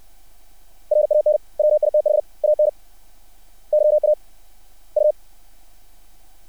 Morse code, heard as a keyed tone, about 60 wpm, spaced as for 35 wpm, 600 Hz. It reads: RAN WITH ANYA U
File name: archived / CRSnet